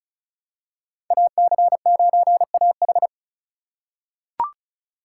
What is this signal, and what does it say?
Morse code, heard as a keyed tone, about 36 wpm, 700 Hz. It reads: AC9AH
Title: USA Call Signs - 5 Characters - 3x Character Spacing - 35wpm